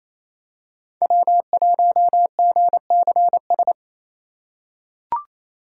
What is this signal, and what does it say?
Morse code, heard as a keyed tone, about 28 wpm, 700 Hz. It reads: W1GCS
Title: K1USN SST QSO Practice - 28wpm